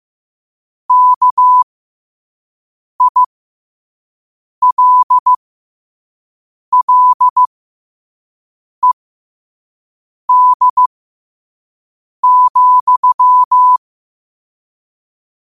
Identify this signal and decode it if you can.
Morse code, heard as a keyed tone, about 15 wpm, spaced as for 5 wpm, 1000 Hz. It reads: KILLED,
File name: qotdfiles / 5wpm